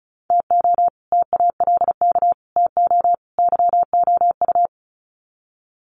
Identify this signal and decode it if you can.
Morse code, heard as a keyed tone, about 34 wpm, 700 Hz. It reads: TO TALK TO YOU
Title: Sets of 4 Words 35wpm